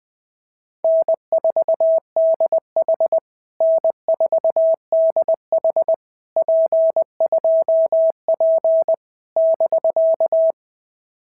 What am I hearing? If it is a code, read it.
Morse code, heard as a keyed tone, about 19 wpm, 650 Hz. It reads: N4DH N4DH P2P <BK>
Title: POTA QSO Practice - 20wpm